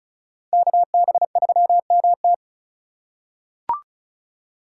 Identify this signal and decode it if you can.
Morse code, heard as a keyed tone, about 36 wpm, 700 Hz. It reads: KB3MT
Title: USA Call Signs - 5 Characters - 4x Character Spacing - 35wpm